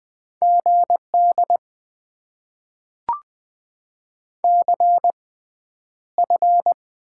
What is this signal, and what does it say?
Morse code, heard as a keyed tone, about 20 wpm, spaced as for 10 wpm, 700 Hz. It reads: GD CF
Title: Top 90 QSO Elements - 6x Character Spacing - 20wpm